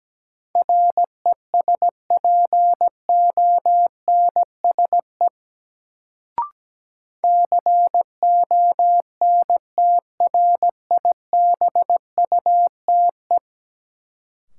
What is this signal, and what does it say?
Morse code, heard as a keyed tone, about 17 wpm, 700 Hz. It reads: RESPONSE CONTRIBUTE
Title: Sentences from Top 1400 Words - New Word Drill - 17wpm